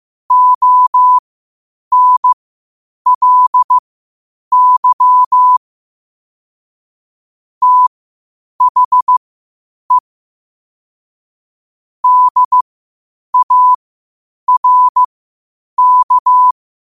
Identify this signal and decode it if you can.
Morse code, heard as a keyed tone, about 15 wpm, spaced as for 8 wpm, 1000 Hz. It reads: ONLY THE DARK
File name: qotdfiles / 8wpm